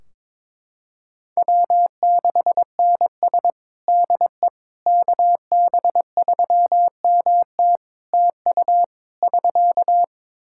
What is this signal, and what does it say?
Morse code, heard as a keyed tone, about 21 wpm, 700 Hz. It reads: W6NS DE KB3MT TU <SK>
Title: Practice QSO - Exchange 4 - Ending - 22wpm